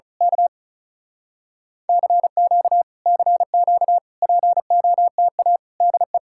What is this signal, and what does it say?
Morse code, heard as a keyed tone, about 32 wpm, 700 Hz. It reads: K CQ CQ POTA DE
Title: Speed-Racing Common QSO Phrases - 35wpm